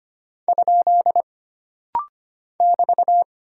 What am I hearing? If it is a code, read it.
Morse code, heard as a keyed tone, about 25 wpm, 700 Hz. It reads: ? <BT>
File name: Single Punctuation-Prosign - Rapid-Fire - 25wpm